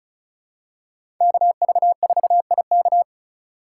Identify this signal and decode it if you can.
Morse code, heard as a keyed tone, about 36 wpm, 700 Hz. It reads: KV4IK
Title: USA Call Signs - 5 Characters - 4x Character Spacing - 35wpm